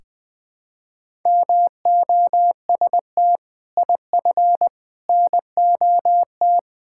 Morse code, heard as a keyed tone, about 20 wpm, 700 Hz. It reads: MOST IF NOT
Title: Sentences with 6-Word Phrases - 20wpm